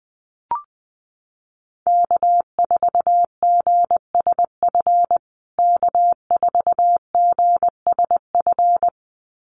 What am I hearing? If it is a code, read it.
Morse code, heard as a keyed tone, about 20 wpm, 700 Hz. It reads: K4GSF K4GSF
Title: USA Call Signs Repeated 2x 20wpm